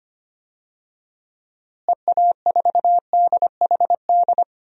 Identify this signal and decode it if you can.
Morse code, heard as a keyed tone, about 25 wpm, 700 Hz. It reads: EA4DHD
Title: DXCC-100 Call Signs - 6 Characters - Repeated 2x - 25wpm